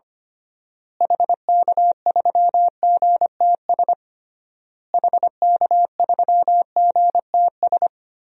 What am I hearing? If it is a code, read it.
Morse code, heard as a keyed tone, about 25 wpm, 700 Hz. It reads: HK3GTS HK3GTS
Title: DXCC-100 Call Signs - 6 Characters - Repeated 3x - 25wpm